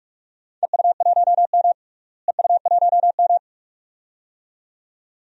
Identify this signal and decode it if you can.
Morse code, heard as a keyed tone, about 46 wpm, 700 Hz. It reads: EU1M EU1M
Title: DXCC-100 Call Signs - 4 Characters - Repeated 2x - 45wpm